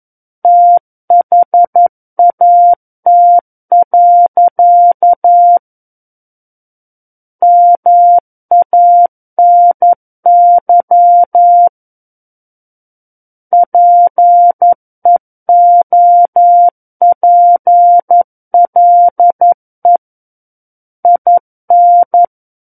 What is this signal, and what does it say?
Morse code, heard as a keyed tone, about 11 wpm, 700 Hz. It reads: THAT. MANY PEOPLE IN
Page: SKCC - War of the Worlds - 02-Chapter 2 - 11 WPM